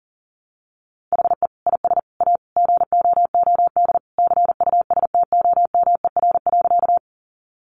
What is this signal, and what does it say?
Morse code, heard as a keyed tone, about 37 wpm, 700 Hz. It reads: HE IS A GOOD CUSTOMER.
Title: Sentences from Top 1100 Words 40wpm